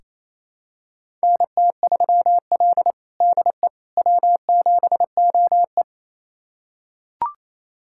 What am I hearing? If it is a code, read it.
Morse code, heard as a keyed tone, about 27 wpm, 690 Hz. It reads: NT3L DE W7OE
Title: USA Callsign DE Callsign - 4 Characters - 28wpm